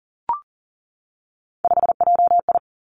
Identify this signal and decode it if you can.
Morse code, heard as a keyed tone, about 38 wpm, 700 Hz. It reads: 5JI
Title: Three Challenging Letter-Number Combinations 40wpm